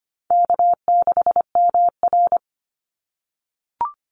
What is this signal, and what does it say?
Morse code, heard as a keyed tone, about 25 wpm, 700 Hz. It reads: K6MR
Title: Easy Call Signs - 25wpm